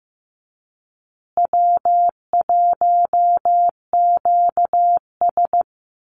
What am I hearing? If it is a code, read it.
Morse code, heard as a keyed tone, about 15 wpm, 700 Hz. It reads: W1QS
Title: Easy Call Signs Repeated 2x - 15wpm